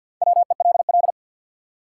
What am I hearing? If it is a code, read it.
Morse code, heard as a keyed tone, about 53 wpm, 700 Hz. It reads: WELL
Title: Top 180 QSO Elements 50wpm